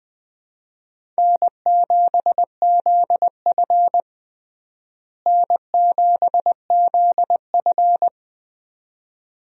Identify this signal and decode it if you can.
Morse code, heard as a keyed tone, about 20 wpm, 700 Hz. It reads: N7ZF N7ZF
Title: USA Call Signs - 4 Characters - Repeated 2x - 20wpm